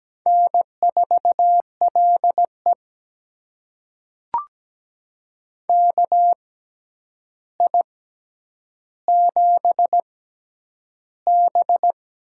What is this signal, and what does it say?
Morse code, heard as a keyed tone, about 17 wpm, spaced as for 9 wpm, 700 Hz. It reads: N4LE KI7B
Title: USA Call Signs - 4 Characters - 6x Character Spacing - 17wpm